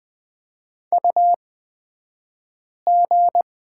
Morse code, heard as a keyed tone, about 20 wpm, spaced as for 5 wpm, 700 Hz. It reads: UG
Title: 242 Lesson 39d - Letter z - Two Characters 20wpm